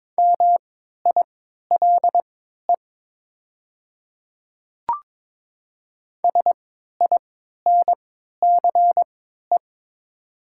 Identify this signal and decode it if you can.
Morse code, heard as a keyed tone, about 21 wpm, spaced as for 12 wpm, 700 Hz. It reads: MILE SINCE
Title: 3x Character Spacing - Top 300 Words - 22wpm